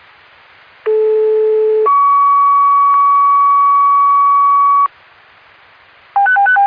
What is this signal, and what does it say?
Two-tone page, heard as a one-second tone, then a three-second tone, about 430 then 1130 Hz. It is heard from 0.9 to 4.9 s.